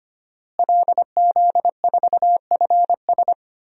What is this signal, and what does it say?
Morse code, heard as a keyed tone, about 25 wpm, 700 Hz. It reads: LZ4FS
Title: DXCC-100 Call Signs - 5 Characters - Repeated 3x - 25wpm